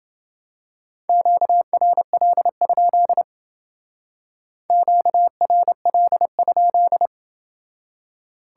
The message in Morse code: QRL? QRL?

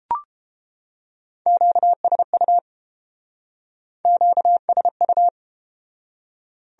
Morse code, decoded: QSU QSU

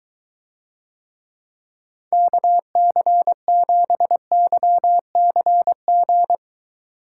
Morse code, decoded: KC7YCG

